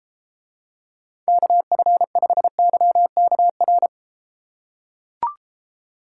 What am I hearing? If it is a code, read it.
Morse code, heard as a keyed tone, about 33 wpm, 700 Hz. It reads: KF5YKR